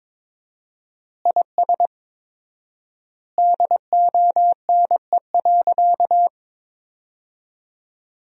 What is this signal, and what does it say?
Morse code, heard as a keyed tone, about 22 wpm, 700 Hz. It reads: IS DONE.
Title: 4x Word-Spacing - Sentences from Top 500 Words - 22wpm